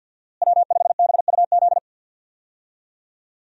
Morse code, heard as a keyed tone, about 47 wpm, 700 Hz. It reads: W5BUZ
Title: USA Call Signs - 5 Characters - Repeated 2x - 50wpm